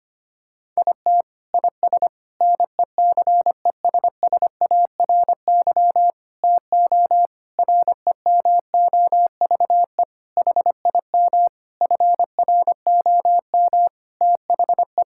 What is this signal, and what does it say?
Morse code, heard as a keyed tone, about 25 wpm, 700 Hz. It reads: IT IS NECESSARY TO REMOVE HIM FROM THE
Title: Sentences from Top 1000 Words - Review Part 5 - 25wpm